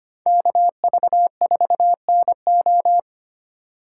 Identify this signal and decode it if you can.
Morse code, heard as a keyed tone, about 25 wpm, 700 Hz. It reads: KV4NO
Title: USA Call Signs - 5 Characters - 3x Character Spacing - 25wpm